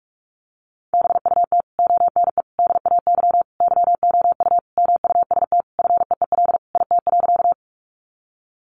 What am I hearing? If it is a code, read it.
Morse code, heard as a keyed tone, about 46 wpm, 700 Hz. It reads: BUT ONE DAY YOU MUST FEEL IT.
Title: Sentences from Top 400 Words 45wpm